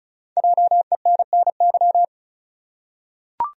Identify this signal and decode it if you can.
Morse code, heard as a keyed tone, about 36 wpm, 700 Hz. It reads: JENNY
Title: USA Common Names - Max 5 Letters - 35wpm